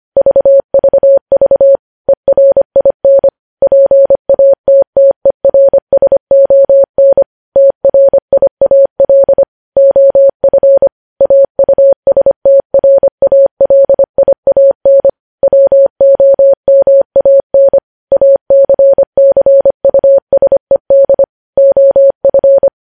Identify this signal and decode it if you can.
Morse code, heard as a keyed tone, about 25 wpm, 550 Hz. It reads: VVV ERIN PATTERSON TRIAL OF AUSTRALIAN WOMAN ACCUSED OF
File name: podcast / News headlines-1x-25x25